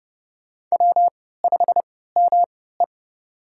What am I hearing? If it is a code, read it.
Morse code, heard as a keyed tone, about 30 wpm, spaced as for 17 wpm, 700 Hz. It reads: W5ME
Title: USA Call Signs - 4 Characters - 3x Character Spacing - 30wpm